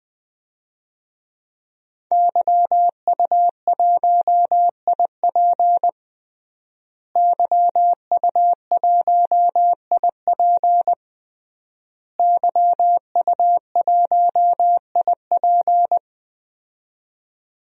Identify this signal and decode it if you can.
Morse code, heard as a keyed tone, about 21 wpm, 700 Hz. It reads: YU1IP YU1IP YU1IP
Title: DXCC-100 Call Signs - 5 Characters - Repeated 3x - 20wpm